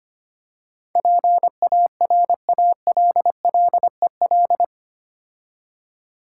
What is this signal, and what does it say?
Morse code, heard as a keyed tone, about 25 wpm, 700 Hz. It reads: PARALLEL
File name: 8 Letter Words - First 3 Letters Given - 25wpm